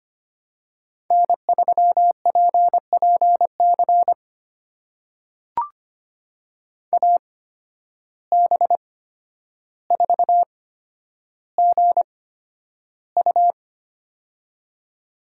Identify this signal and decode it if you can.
Morse code, heard as a keyed tone, about 25 wpm, spaced as for 11 wpm, 700 Hz. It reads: N3PPC AB4GU
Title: USA Call Signs - 5 Characters - 8x Character Spacing - 25wpm